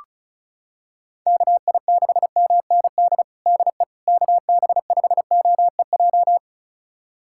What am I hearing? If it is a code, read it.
Morse code, heard as a keyed tone, about 34 wpm, 700 Hz. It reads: KI6MND DE KB5OEJ
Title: USA Callsign DE Callsign 35wpm